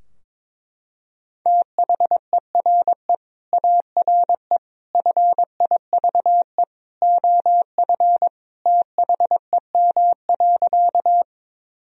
Morse code, heard as a keyed tone, about 22 wpm, 700 Hz. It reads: THERE ARE FIVE OF THEM.
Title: Sentences from Top 1000 Words - Review Part 2 - 22wpm